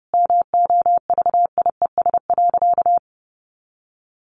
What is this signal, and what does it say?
Morse code, heard as a keyed tone, about 30 wpm, 700 Hz. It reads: MOVIES.
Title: Additional Easy Phrases 30wpm